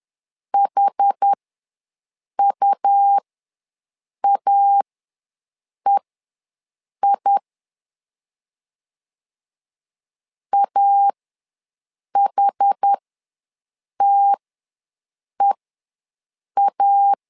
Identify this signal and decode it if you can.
Morse code, heard as a keyed tone, about 10 wpm, spaced as for 6 wpm, 800 Hz. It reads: HUAEI AHTEA
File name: LEZIONI / 2 UHDN / 2 UH ws 10wpm 2029